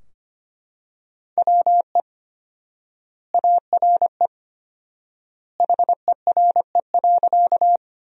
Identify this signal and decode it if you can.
Morse code, heard as a keyed tone, about 25 wpm, 700 Hz. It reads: WE ARE HERE.
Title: 4x Word-Spacing - Sentences from Top 200 Words - 25wpm